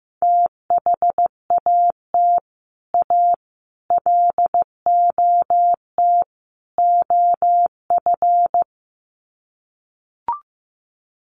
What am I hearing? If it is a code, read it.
Morse code, heard as a keyed tone, about 15 wpm, 700 Hz. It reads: THAT A LOT OF
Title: Sets of 4 Words 15wpm